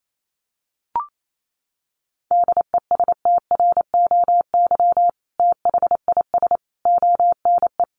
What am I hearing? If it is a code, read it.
Morse code, heard as a keyed tone, about 28 wpm, 700 Hz. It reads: DESTROY THIS ONE
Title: A Sound Of Thunder 28wpm